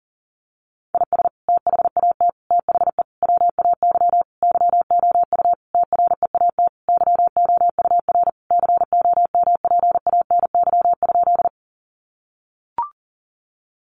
Morse code, heard as a keyed tone, about 40 wpm, 700 Hz. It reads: IS THAT THE WAY YOU TREAT YOUR COMPANY?
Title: Sentences from Top 1300 Words 40wpm